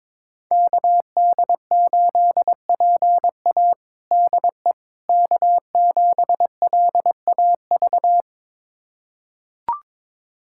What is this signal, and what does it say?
Morse code, heard as a keyed tone, about 22 wpm, 700 Hz. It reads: KD8PA DE K7LAV
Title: USA Callsign DE Callsign - 5 Characters - 22wpm